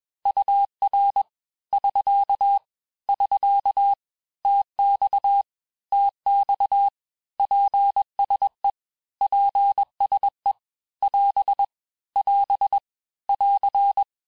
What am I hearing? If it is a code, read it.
Morse code, heard as a keyed tone, about 21 wpm, 800 Hz. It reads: UR <SK> <SK> TX TX PSE PSE <AS> <AS> <AR>
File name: cours-du-lundi files / cours-1592--teste-abreviations-vitesse-15mpm-puis-20mpm